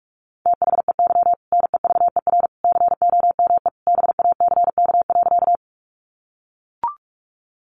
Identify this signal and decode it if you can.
Morse code, heard as a keyed tone, about 44 wpm, 700 Hz. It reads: THEY NEVER COME BACK.
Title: Sentences from Top 300 Words 45wpm